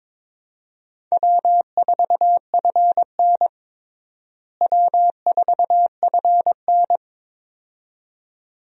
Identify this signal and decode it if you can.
Morse code, heard as a keyed tone, about 22 wpm, 700 Hz. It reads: W4FN W4FN
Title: USA Call Signs - 4 Characters - Repeated 2x - 22wpm